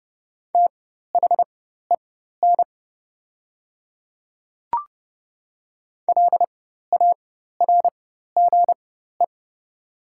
Morse code, heard as a keyed tone, about 30 wpm, spaced as for 14 wpm, 700 Hz. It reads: THEN LARGE
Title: 4x Character Spacing - Top 200 Words - 30wpm